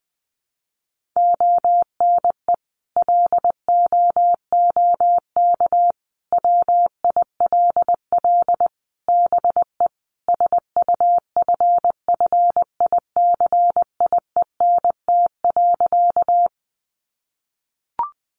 Morse code, heard as a keyed tone, about 19 wpm, 700 Hz. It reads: ONE LOOK WILL BE SUFFICIENT.
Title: Sentences from Top 1400 Words 20wpm